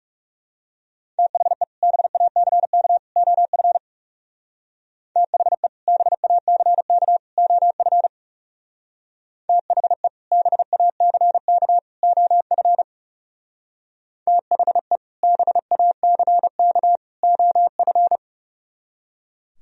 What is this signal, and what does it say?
Morse code, heard as a keyed tone, about 37 wpm, 700 Hz. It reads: THE BACK OF THE BACK OF THE BACK OF THE BACK OF
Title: Speed-Racing - Sets of 3 Words 30wpm